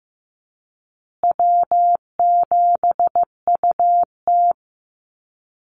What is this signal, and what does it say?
Morse code, heard as a keyed tone, about 15 wpm, 700 Hz. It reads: W7UT
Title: Easy Call Signs Repeated 3x - 15wpm